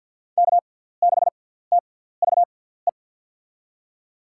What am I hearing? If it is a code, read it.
Morse code, heard as a keyed tone, about 50 wpm, spaced as for 17 wpm, 700 Hz. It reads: K6TVE
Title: USA Call Signs - 5 Characters - 6x Character Spacing - 50wpm